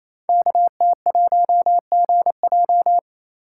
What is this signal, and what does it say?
Morse code, heard as a keyed tone, about 27 wpm, 700 Hz. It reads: KT1GJ
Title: Code Groups - 5 Letter:Numbers - 28wpm